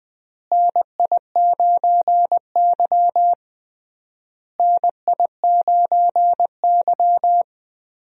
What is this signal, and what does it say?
Morse code, heard as a keyed tone, about 20 wpm, 700 Hz. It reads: NI9Y NI9Y
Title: USA Call Signs - 4 Characters - Repeated 3x - 20wpm